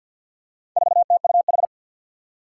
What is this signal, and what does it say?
Morse code, heard as a keyed tone, about 47 wpm, 700 Hz. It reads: 4TUH
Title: Four Challenging Letter-Number Combinations 50wpm